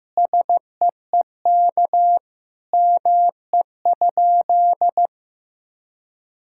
Morse code, heard as a keyed tone, about 16 wpm, 700 Hz. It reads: SEEK ME?
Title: Sentences from Top 1600 Words 15wpm